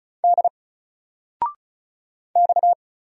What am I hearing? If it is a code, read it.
Morse code, heard as a keyed tone, about 36 wpm, 700 Hz. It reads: D X
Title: Single Letter-Number 35wpm